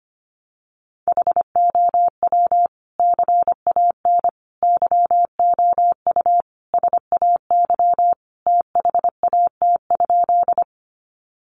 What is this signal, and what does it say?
Morse code, heard as a keyed tone, about 25 wpm, 700 Hz. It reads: HOW CAN YOU SAY THAT?
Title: Sentences from Top 200 Words 25wpm